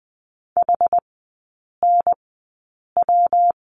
Morse code, heard as a keyed tone, about 20 wpm, spaced as for 8 wpm, 700 Hz. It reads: HNW
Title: 138 Lesson 23e - Letter f - Three Characters 20wpm